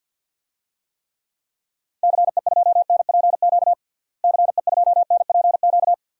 Morse code, heard as a keyed tone, about 48 wpm, 700 Hz. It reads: CE2NPQ CE2NPQ
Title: DXCC-100 Call Signs - 6 Characters - Repeated 3x - 50wpm